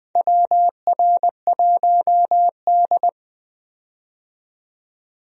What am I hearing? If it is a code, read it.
Morse code, heard as a keyed tone, about 20 wpm, 700 Hz. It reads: WR1D
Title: USA Call Signs - 4 Characters - 20wpm